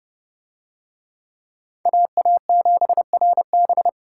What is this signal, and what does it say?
Morse code, heard as a keyed tone, about 26 wpm, 700 Hz. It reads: AA7RB